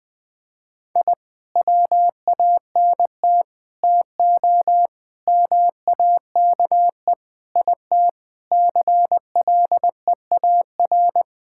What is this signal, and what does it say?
Morse code, heard as a keyed tone, about 21 wpm, 700 Hz. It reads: I WANT TO MAKE IT CLEAR
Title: Sentences with 7-Word Phrases - 20wpm